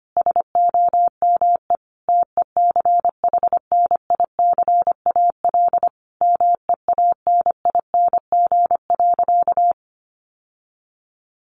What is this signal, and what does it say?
Morse code, heard as a keyed tone, about 25 wpm, 700 Hz. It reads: SOME TECHNICAL MEANING.
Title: Sentences from Top 1100 Words 25wpm